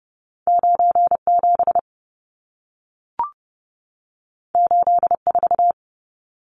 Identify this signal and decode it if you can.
Morse code, heard as a keyed tone, about 30 wpm, 700 Hz. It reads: 97 84
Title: Numbers - 2-Digit 30wpm